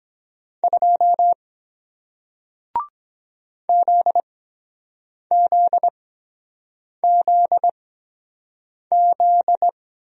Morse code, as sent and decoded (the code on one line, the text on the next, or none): ..--- / --.. --.. --.. --..
2 ZZZZ